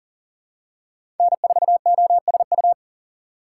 Morse code, heard as a keyed tone, about 40 wpm, 700 Hz. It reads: N4OSU
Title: USA Call Signs - 5 Characters - 2x Character Spacing - 40wpm